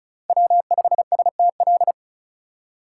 Morse code, heard as a keyed tone, about 36 wpm, 700 Hz. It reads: W5STL
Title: USA Call Signs - 5 Characters - 4x Character Spacing - 35wpm